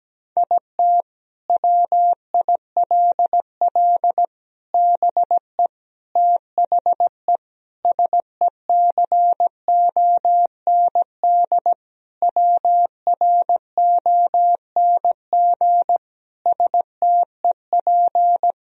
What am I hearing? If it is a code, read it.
Morse code, heard as a keyed tone, about 17 wpm, 700 Hz. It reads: IT WILL BE THE SECOND WRONG STEP
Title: Sentences from Top 1000 Words - Review Part 2 - 17wpm